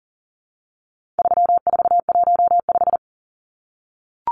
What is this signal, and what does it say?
Morse code, heard as a keyed tone, about 40 wpm, 700 Hz. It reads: 3415
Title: Numbers - 4-Digit 40wpm